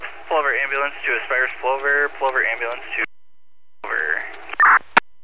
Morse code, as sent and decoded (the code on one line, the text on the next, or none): none